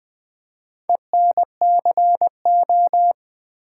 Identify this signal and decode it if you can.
Morse code, heard as a keyed tone, about 20 wpm, 700 Hz. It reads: ENCO
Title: Build A Word - 4 Letters At A Time - 11-Letter Words - 20wpm